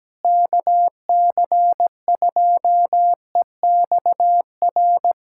Morse code, heard as a keyed tone, about 17 wpm, 700 Hz. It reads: KC2EXR
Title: USA Callsign DE Callsign 17wpm